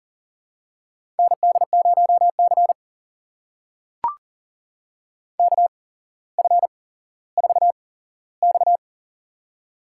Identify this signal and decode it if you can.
Morse code, heard as a keyed tone, about 41 wpm, spaced as for 17 wpm, 700 Hz. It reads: ND0C KF4X
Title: USA Call Signs - 4 Characters - 8x Character Spacing - 40wpm